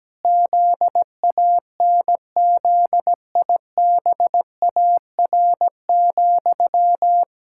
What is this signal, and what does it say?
Morse code, heard as a keyed tone, about 17 wpm, 700 Hz. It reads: ZANZIBAR,
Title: Pangram Challenge v2 - 17wpm